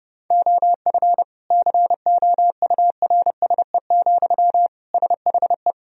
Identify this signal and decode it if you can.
Morse code, heard as a keyed tone, about 30 wpm, 700 Hz. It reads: OF COURSE, SHE
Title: Sentences from Top 1000 Words - Review Part 4 - 30wpm